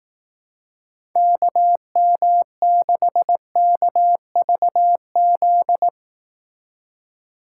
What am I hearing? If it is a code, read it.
Morse code, heard as a keyed tone, about 18 wpm, 700 Hz. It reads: KM6KVZ
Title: Speed Racing - USA 2x3 Call Signs 15wpm